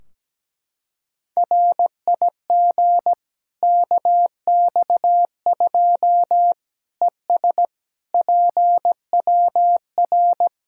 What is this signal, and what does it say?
Morse code, heard as a keyed tone, about 17 wpm, 700 Hz. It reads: RIG KX2 ES PWR
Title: Practice QSO - Exchange 2 - Info - 17wpm